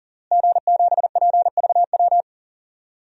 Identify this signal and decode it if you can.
Morse code, heard as a keyed tone, about 40 wpm, 700 Hz. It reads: G7PVW
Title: DXCC-100 Call Signs - 5 Characters - Repeated 3x - 40wpm